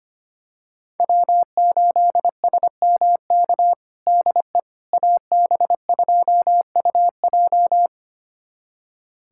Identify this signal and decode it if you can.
Morse code, heard as a keyed tone, about 25 wpm, 700 Hz. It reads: W8SMK DE AB2UJ
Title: USA Callsign DE Callsign - 5 Characters - 25wpm